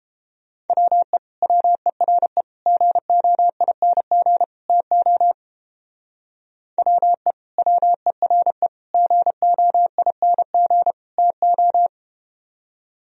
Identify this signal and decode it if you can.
Morse code, heard as a keyed tone, about 30 wpm, 700 Hz. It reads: WE WERE GOING TO WE WERE GOING TO